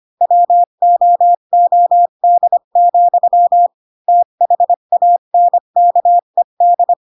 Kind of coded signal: Morse code